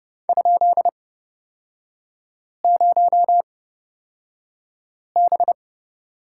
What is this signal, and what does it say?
Morse code, heard as a keyed tone, about 29 wpm, spaced as for 5 wpm, 690 Hz. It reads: ?0B
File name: Lesson 37e - Letter x - Three Characters 30wpm